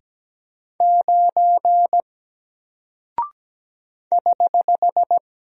Morse code, heard as a keyed tone, about 17 wpm, 700 Hz. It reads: 9 <HH>
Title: Single Character (Letter:Number:Punctuation:Prosign) 17wpm